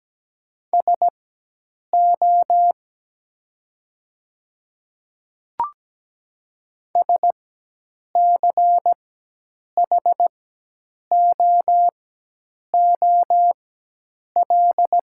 Morse code, heard as a keyed tone, about 17 wpm, spaced as for 8 wpm, 700 Hz. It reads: SO SCHOOL
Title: 4x Character Spacing - Top 500 Words - 17wpm